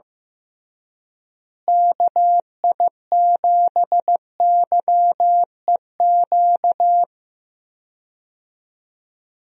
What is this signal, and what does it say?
Morse code, heard as a keyed tone, about 15 wpm, 700 Hz. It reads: KI7YEQ
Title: USA Call Signs - 6 Characters - Repeated 3x - 15wpm